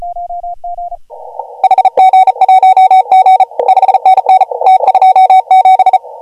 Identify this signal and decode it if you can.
Morse code, heard as a keyed tone, about 33 wpm, 750 Hz. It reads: HG1G 5NN T27
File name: AUDIO2019 / HG1G CW